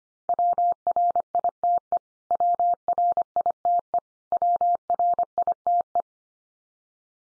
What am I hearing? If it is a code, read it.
Morse code, heard as a keyed tone, about 24 wpm, 700 Hz. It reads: WRITE WRITE WRITE
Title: Top 100 Words 25wpm x3